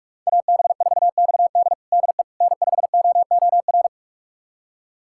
Morse code, heard as a keyed tone, about 46 wpm, 700 Hz. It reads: AB4XD DE N5OOR
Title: USA Callsign DE Callsign - 5 Characters - 45wpm